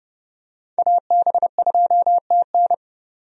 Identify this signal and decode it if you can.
Morse code, heard as a keyed tone, about 30 wpm, 700 Hz. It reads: AB2TN
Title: USA Call Signs - 5 Characters - 30wpm